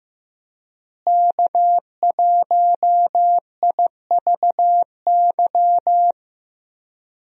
Morse code, heard as a keyed tone, about 15 wpm, 700 Hz. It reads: K1IVY
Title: USA Call Signs - 5 Characters - Repeated 2x - 15wpm